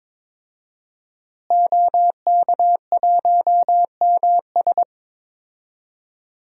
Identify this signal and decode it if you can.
Morse code, heard as a keyed tone, about 22 wpm, 700 Hz. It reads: OK1MS
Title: DXCC-100 Call Signs - 5 Characters - Repeated 2x - 22wpm